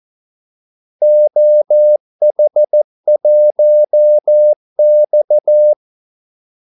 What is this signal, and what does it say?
Morse code, heard as a keyed tone, about 14 wpm, 600 Hz. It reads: OH1X
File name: Callsigns-14wpm